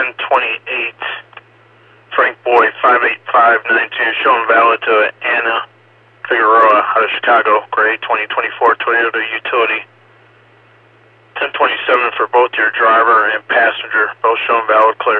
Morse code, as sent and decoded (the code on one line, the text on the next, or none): none